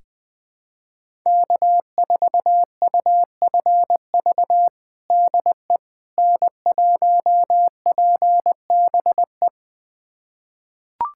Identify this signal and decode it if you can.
Morse code, heard as a keyed tone, about 21 wpm, 700 Hz. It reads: K4UFV DE N1PBE
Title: USA Callsign DE Callsign - 5 Characters - 20wpm